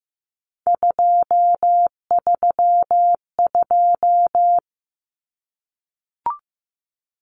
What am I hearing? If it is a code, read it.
Morse code, heard as a keyed tone, about 15 wpm, 700 Hz. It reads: 232
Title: Numbers - 3-Digit 15wpm